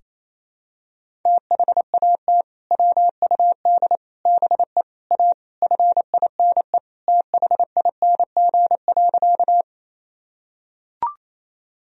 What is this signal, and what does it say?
Morse code, heard as a keyed tone, about 28 wpm, 700 Hz. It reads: THAT WUD BE A FINE THING.
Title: Sentences from Top 500 Words and Abbreviations 28wpm